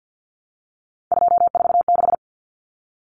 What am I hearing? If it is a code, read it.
Morse code, heard as a keyed tone, about 50 wpm, 700 Hz. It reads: 246
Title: Numbers - 3-Digit 50wpm